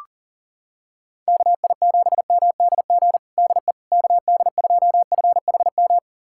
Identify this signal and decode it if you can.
Morse code, heard as a keyed tone, about 37 wpm, 700 Hz. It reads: KI7MDG DE KD2FHM
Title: USA Callsign DE Callsign 40wpm